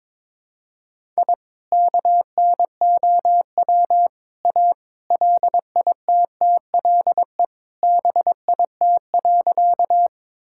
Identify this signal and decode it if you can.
Morse code, heard as a keyed tone, about 22 wpm, 700 Hz. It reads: I KNOW A LITTLE BIT.
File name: practice / 22wpm / Sentences with 3-Word Phrases - 22wpm